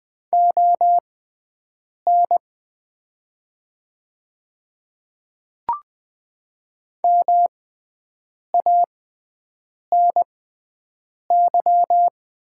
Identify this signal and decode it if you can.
Morse code, heard as a keyed tone, about 20 wpm, spaced as for 7 wpm, 700 Hz. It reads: ON MANY
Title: 6x Character Spacing - Top 300 Words - 20wpm